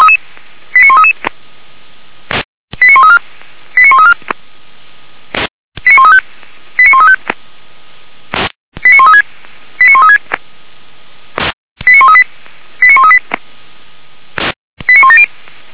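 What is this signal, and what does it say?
Five-tone selective call (ZVEI), heard as quick runs of five tones, tones of about 70 ms. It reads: 89133 89134 89134 89135 89135 89136 89136 89138 89138 89170
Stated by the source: ZVEI five-tone selective call received on an SDR, recorded by Feuerwehr Orlamünde (f85360).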